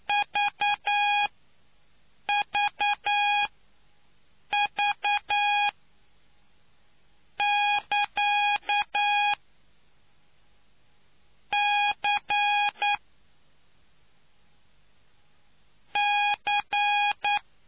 Morse code, heard as a keyed tone, about 9 wpm, spaced as for 4 wpm, 820 Hz. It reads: VVV<KA>C C